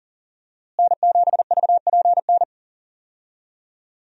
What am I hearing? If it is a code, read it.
Morse code, heard as a keyed tone, about 40 wpm, 700 Hz. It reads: N7VPN
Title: USA Call Signs - 5 Characters - Repeated 2x - 40wpm